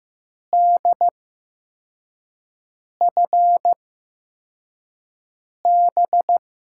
Morse code, heard as a keyed tone, about 15 wpm, spaced as for 4 wpm, 700 Hz. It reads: DFB
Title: USA Call Signs - 6 Characters - 8x Character Spacing - 15wpm